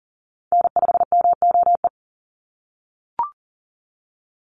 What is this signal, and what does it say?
Morse code, heard as a keyed tone, about 40 wpm, 700 Hz. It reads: N5MOE
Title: USA Call Signs 40wpm